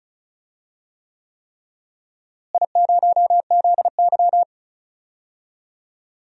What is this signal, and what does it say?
Morse code, heard as a keyed tone, about 36 wpm, 700 Hz. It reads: I0ZY
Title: DXCC-100 Call Signs - 4 Characters - Repeated 3x - 35wpm